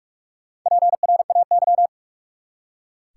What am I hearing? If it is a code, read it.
Morse code, heard as a keyed tone, about 42 wpm, 700 Hz. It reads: PRAY
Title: Four Letter Words - 45wpm